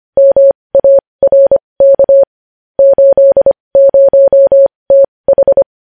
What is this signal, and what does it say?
Morse code, heard as a keyed tone, about 25 wpm, spaced as for 20 wpm, 550 Hz. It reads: MARK 80TH
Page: news-headlines-in-morse-code-at-20-wpm